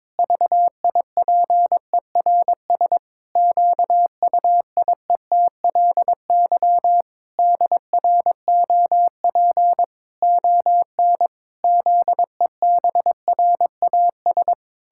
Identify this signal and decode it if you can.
Morse code, heard as a keyed tone, about 22 wpm, 700 Hz. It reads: VIPERS QUIETLY DROP ON ZEBRAS
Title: Pangram Challenge v2 - 22wpm